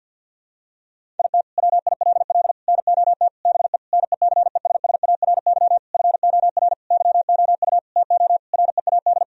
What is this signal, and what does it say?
Morse code, heard as a keyed tone, about 51 wpm, 700 Hz. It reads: IT WILL NOT BE NECESSARY FOR YOU TO READ
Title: Sentences from Top 1000 Words - Review Part 1 - 50wpm